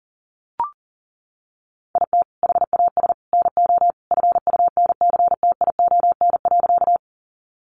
Morse code, heard as a keyed tone, about 40 wpm, 700 Hz. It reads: IT HAS NO FUNCTION.
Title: Sentences from Top 1100 Words 40wpm